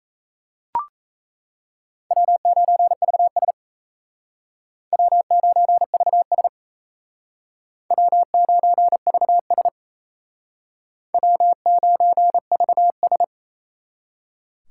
Morse code, heard as a keyed tone, about 36 wpm, 700 Hz. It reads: W9VS W9VS W9VS W9VS